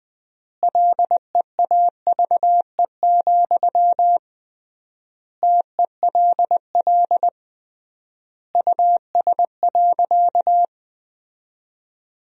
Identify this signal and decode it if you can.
Morse code, heard as a keyed tone, about 20 wpm, 700 Hz. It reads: LEAVE, TELL US.
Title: 3x Word-Spacing - Sentences from Top 400 Words - 20wpm